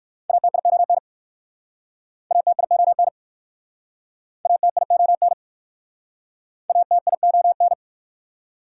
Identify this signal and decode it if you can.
Morse code, heard as a keyed tone, about 60 wpm, 690 Hz. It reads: ATION ATION ATION ATION